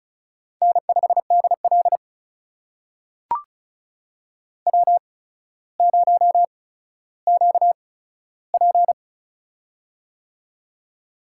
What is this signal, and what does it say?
Morse code, heard as a keyed tone, about 36 wpm, spaced as for 15 wpm, 700 Hz. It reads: N5DL W0QP
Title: USA Call Signs - 4 Characters - 8x Character Spacing - 35wpm